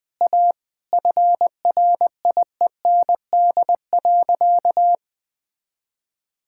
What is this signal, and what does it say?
Morse code, heard as a keyed tone, about 20 wpm, 700 Hz. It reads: A FRIEND.
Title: Sentences with 2-Word Phrases - 20wpm